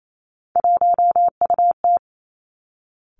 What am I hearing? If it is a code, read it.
Morse code, heard as a keyed tone, about 28 wpm, 700 Hz. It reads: 1UT
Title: Three Challenging Letter-Number Combinations 28wpm